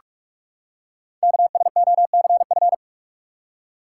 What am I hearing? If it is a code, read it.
Morse code, heard as a keyed tone, about 44 wpm, 700 Hz. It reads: KSOCF
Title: Code Groups - 5 Letter:Numbers - 45wpm